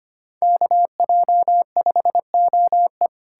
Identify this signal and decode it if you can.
Morse code, heard as a keyed tone, about 33 wpm, 700 Hz. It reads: KJ5OE